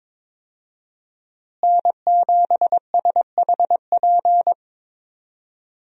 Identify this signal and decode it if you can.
Morse code, heard as a keyed tone, about 22 wpm, 700 Hz. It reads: N7SHP
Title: USA Call Signs - 5 Characters - Repeated 2x - 22wpm